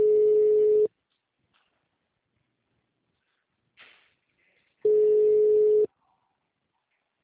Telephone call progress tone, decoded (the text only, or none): ringback tone